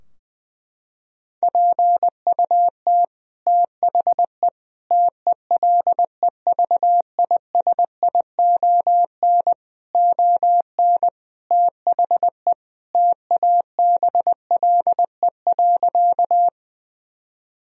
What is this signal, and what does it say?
Morse code, heard as a keyed tone, about 20 wpm, 700 Hz. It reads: PUT THE TELEVISION ON THE TABLE.
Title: Sentences from Top 1000 Words - Review Part 4 - 20wpm